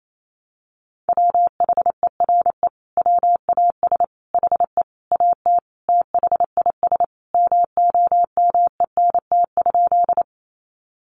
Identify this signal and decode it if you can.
Morse code, heard as a keyed tone, about 28 wpm, 700 Hz. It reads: WHERE WAS HE AT THIS MOMENT?
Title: Sentences from Top 1000 Words 28wpm